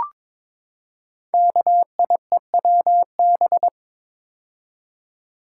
Morse code, heard as a keyed tone, about 23 wpm, 700 Hz. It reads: KIEWB
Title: Code Groups - 5 Letters - 22wpm